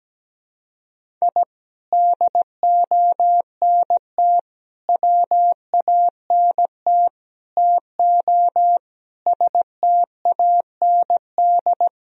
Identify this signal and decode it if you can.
Morse code, heard as a keyed tone, about 17 wpm, 700 Hz. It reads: I DONT WANT TO STAND
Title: Sentences from Top 1000 Words - Review Part 1 - 17wpm